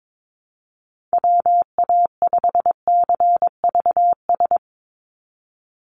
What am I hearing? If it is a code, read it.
Morse code, heard as a keyed tone, about 22 wpm, 700 Hz. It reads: WA5CVS
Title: USA Call Signs 22wpm